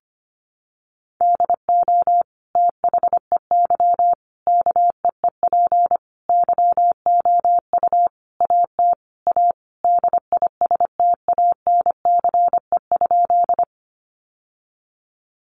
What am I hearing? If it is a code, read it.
Morse code, heard as a keyed tone, about 25 wpm, 700 Hz. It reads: DO THEY KEEP YOU AT A DISTANCE?
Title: Sentences from Top 1200 Words 25wpm